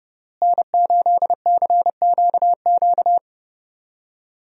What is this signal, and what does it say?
Morse code, heard as a keyed tone, about 30 wpm, 700 Hz. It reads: N8CQQ
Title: USA Call Signs - 5 Characters - Repeated 3x - 30wpm